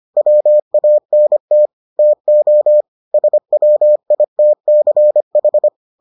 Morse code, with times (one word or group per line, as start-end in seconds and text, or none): 0.2-1.7 WANT
2.0-2.8 TO
3.1-5.7 SWITCH